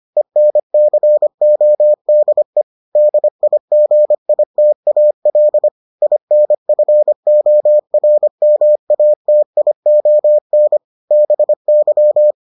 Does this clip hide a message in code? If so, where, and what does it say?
Yes: Morse code from 0.2 to 12.3 s, heard as a keyed tone, about 25 wpm, 600 Hz. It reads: ENCODE DIGITAL INFORMATION BY